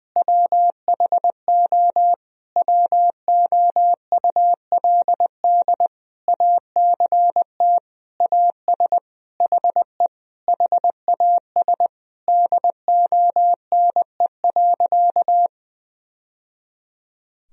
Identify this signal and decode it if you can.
Morse code, heard as a keyed tone, about 20 wpm, 700 Hz. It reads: WHO WOULD ACT AS HE HAS DONE.
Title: Sentences from Top 1000 Words - Review Part 1 - 20wpm